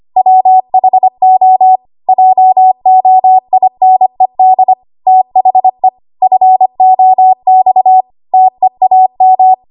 Morse code, heard as a keyed tone, about 25 wpm, 750 Hz. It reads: WHO JOINED THE FOX TEAM
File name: Archive / 25 WPM / 25WPM